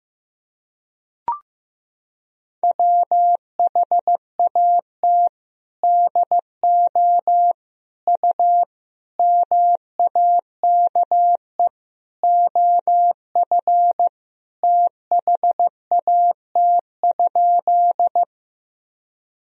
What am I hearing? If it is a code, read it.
Morse code, heard as a keyed tone, about 15 wpm, 700 Hz. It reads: WHAT DO U MAKE OF THAT?
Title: Sentences from Top 100 Words and Abbreviations 15wpm